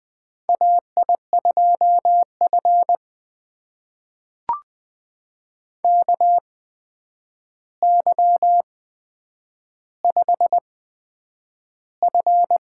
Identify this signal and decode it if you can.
Morse code, heard as a keyed tone, about 20 wpm, spaced as for 9 wpm, 700 Hz. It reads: AI2F KY5F
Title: USA Call Signs - 4 Characters - 8x Character Spacing - 20wpm